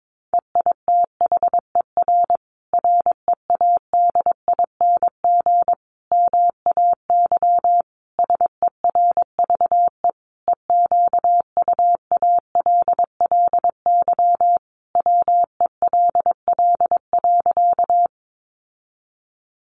Morse code, heard as a keyed tone, about 22 wpm, 700 Hz. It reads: EITHER READING MAY SERVE EQUALLY WELL.
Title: Sentences from Top 1400 Words 22wpm